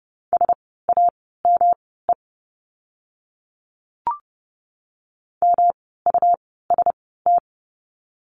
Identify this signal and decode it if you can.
Morse code, heard as a keyed tone, about 30 wpm, spaced as for 17 wpm, 700 Hz. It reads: SAME MUST
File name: course / 30:10wpm / 115 Lesson 19f - Letter w - Words 30wpm